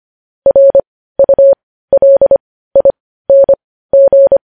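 Morse code, heard as a keyed tone, about 25 wpm, spaced as for 15 wpm, 550 Hz. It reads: RULING
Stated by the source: Morse practice sending news headlines at 15 wpm.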